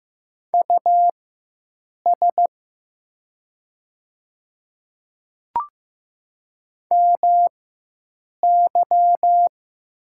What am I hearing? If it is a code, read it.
Morse code, heard as a keyed tone, about 15 wpm, spaced as for 7 wpm, 700 Hz. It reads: US MY
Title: 4x Character Spacing - Top 200 Words - 15wpm